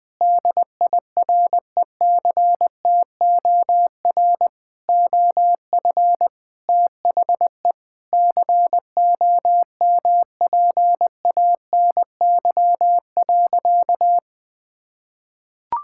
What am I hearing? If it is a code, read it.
Morse code, heard as a keyed tone, about 21 wpm, 700 Hz. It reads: DIRECTOR OF THE COMPANY.
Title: Sentences with 3-Word Phrases - Spoken Beginning - 20wpm